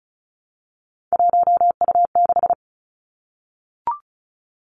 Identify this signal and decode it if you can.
Morse code, heard as a keyed tone, about 36 wpm, 700 Hz. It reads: 1U6
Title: Three Challenging Letter-Number Combinations 35wpm